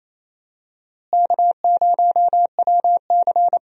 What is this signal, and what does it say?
Morse code, heard as a keyed tone, about 28 wpm, 700 Hz. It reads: K0WC
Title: USA Call Signs - 4 Characters - Repeated 3x - 28wpm